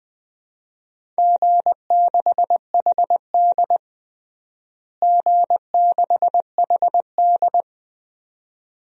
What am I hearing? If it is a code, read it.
Morse code, heard as a keyed tone, about 20 wpm, 700 Hz. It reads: G6HD G6HD
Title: DXCC-100 Call Signs - 4 Characters - Repeated 2x - 20wpm